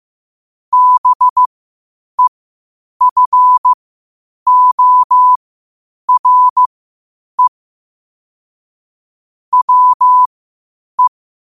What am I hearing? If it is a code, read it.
Morse code, heard as a keyed tone, about 15 wpm, spaced as for 8 wpm, 1000 Hz. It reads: BEFORE WE
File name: qotdfiles / 8wpm